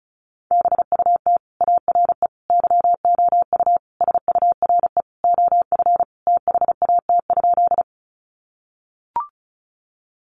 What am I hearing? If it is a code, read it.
Morse code, heard as a keyed tone, about 34 wpm, 700 Hz. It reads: BUT ARE YOU SURE OF THAT?
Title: Sentences from Top 300 Words 35wpm